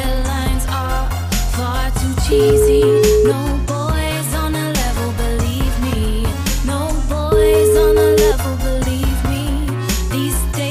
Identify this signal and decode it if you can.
Ringback tone (Europe, 425 Hz), with noise.